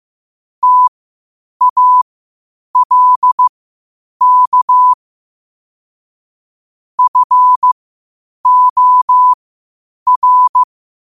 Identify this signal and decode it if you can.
Morse code, heard as a keyed tone, about 15 wpm, spaced as for 8 wpm, 1000 Hz. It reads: TALK FOR
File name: 8wpm